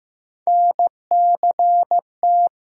Morse code, heard as a keyed tone, about 15 wpm, 700 Hz. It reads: NCT
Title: USA Call Sign Postfixes - 3 Letters 15wpm